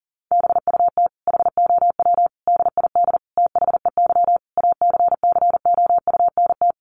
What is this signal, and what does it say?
Morse code, heard as a keyed tone, about 40 wpm, 700 Hz. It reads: BUT HOW DID THEY ACCOUNT
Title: Sentences from Top 900 Words 40wpm